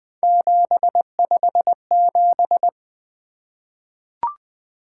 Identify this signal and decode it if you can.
Morse code, heard as a keyed tone, about 20 wpm, 700 Hz. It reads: 757
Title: Three Letter-Number Combinations - 20wpm